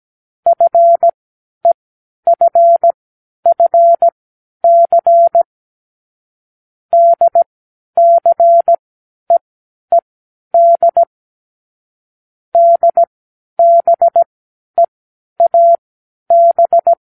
Morse code, heard as a keyed tone, about 17 wpm, spaced as for 10 wpm, 680 Hz. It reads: FEFFC DCEED DBEAB
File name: k7qo / Lesson9(A-F)